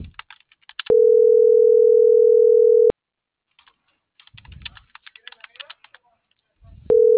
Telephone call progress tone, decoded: ringback tone

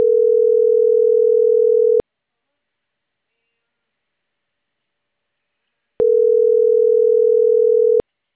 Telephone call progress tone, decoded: ringback tone

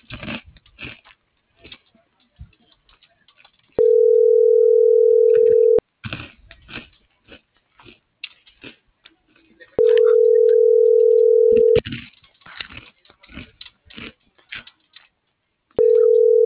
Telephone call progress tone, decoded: ringback tone

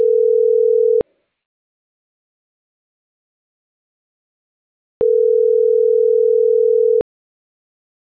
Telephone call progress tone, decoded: ringback tone

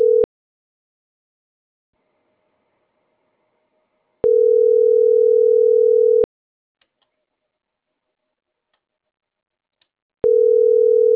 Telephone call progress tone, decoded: ringback tone

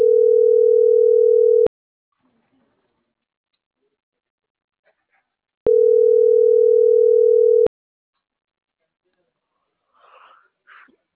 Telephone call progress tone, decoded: ringback tone

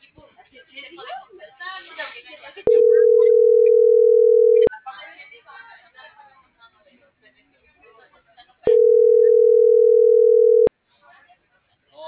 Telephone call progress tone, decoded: ringback tone

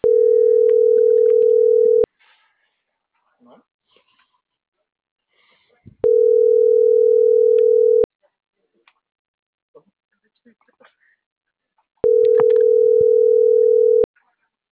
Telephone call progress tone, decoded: ringback tone

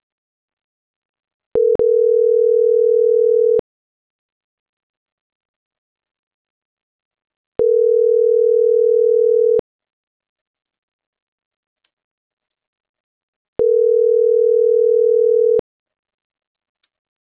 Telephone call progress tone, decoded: ringback tone

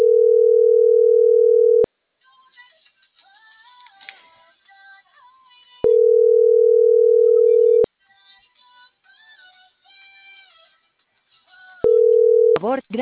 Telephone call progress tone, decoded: ringback tone

